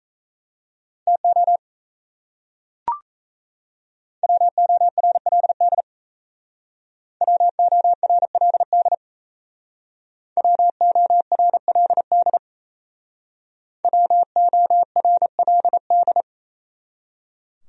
Morse code, heard as a keyed tone, about 36 wpm, 700 Hz. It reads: TO WORLD WORLD WORLD WORLD